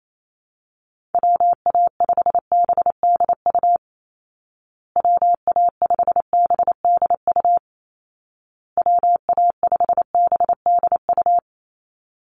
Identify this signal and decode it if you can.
Morse code, heard as a keyed tone, about 28 wpm, 700 Hz. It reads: WA5BDU WA5BDU WA5BDU
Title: Call Signs Repeated 2x - 28wpm